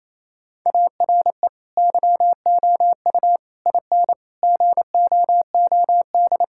Morse code, heard as a keyed tone, about 28 wpm, 700 Hz. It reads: ARE YOU IN GOOD
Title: Sentences from Top 1000 Words - Review Part 5 - 28wpm